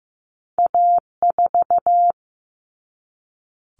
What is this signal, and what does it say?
Morse code, heard as a keyed tone, about 15 wpm, 700 Hz. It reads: A4
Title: Common Cut Numbers - 2-Digit 15wpm